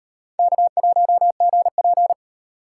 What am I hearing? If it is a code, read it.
Morse code, heard as a keyed tone, about 36 wpm, 700 Hz. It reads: K1GP